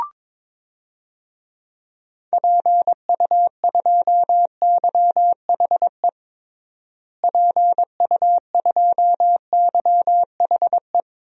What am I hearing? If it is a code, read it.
Morse code, heard as a keyed tone, about 22 wpm, 700 Hz. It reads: PU2YHE PU2YHE
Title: DXCC-100 Call Signs - 6 Characters - Repeated 2x - 22wpm